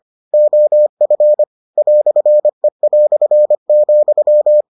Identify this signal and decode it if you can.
Morse code, heard as a keyed tone, about 25 wpm, 600 Hz. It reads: OF "E",